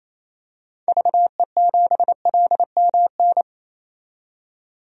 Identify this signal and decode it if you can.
Morse code, heard as a keyed tone, about 28 wpm, 700 Hz. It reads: VE7LMN
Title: DXCC-100 Call Signs - 6 Characters - 28wpm